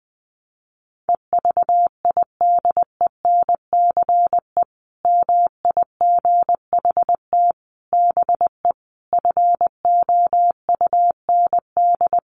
Morse code, heard as a keyed tone, about 20 wpm, 700 Hz. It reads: EVIDENCE MIGHT BE FOUND
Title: Sentences from Top 1300 Words 20wpm